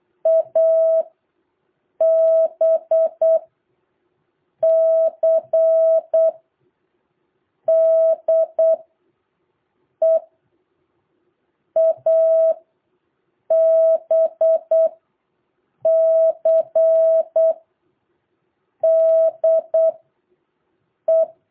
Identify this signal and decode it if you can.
Morse code, heard as a keyed tone, about 8 wpm, spaced as for 5 wpm, 650 Hz. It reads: ABCDE ABCDE
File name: jh8cbh / 2023cw